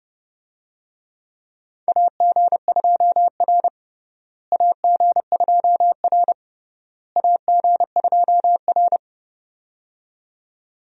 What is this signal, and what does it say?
Morse code, heard as a keyed tone, about 31 wpm, 700 Hz. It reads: AG2R AG2R AG2R
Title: USA Call Signs - 4 Characters - Repeated 3x - 30wpm